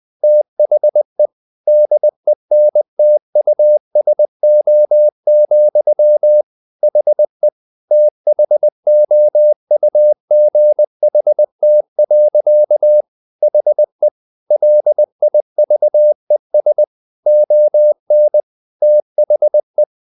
Morse code, heard as a keyed tone, about 20 wpm, 600 Hz. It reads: THE DENTUSO, HE THOUGHT. HE LIVES ON THE
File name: the old man and the sea part 2 20wpm